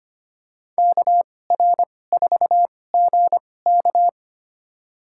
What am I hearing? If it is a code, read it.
Morse code, heard as a keyed tone, about 25 wpm, spaced as for 18 wpm, 700 Hz. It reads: KR4GK
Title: USA Call Signs - 5 Characters - 2x Character Spacing - 25wpm